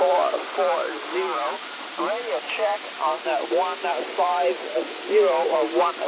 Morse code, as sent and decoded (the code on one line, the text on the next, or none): none